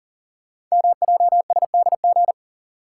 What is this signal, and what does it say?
Morse code, heard as a keyed tone, about 39 wpm, 700 Hz. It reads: MJSDG
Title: Code Groups - 5 Letters - 40wpm